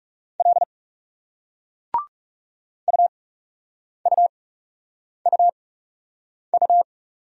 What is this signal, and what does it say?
Morse code, heard as a keyed tone, about 40 wpm, spaced as for 8 wpm, 700 Hz. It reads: L UUUU